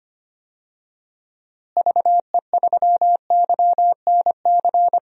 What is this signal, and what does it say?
Morse code, heard as a keyed tone, about 25 wpm, 700 Hz. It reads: VE3YNC
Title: DXCC-100 Call Signs - 6 Characters - Repeated 2x - 25wpm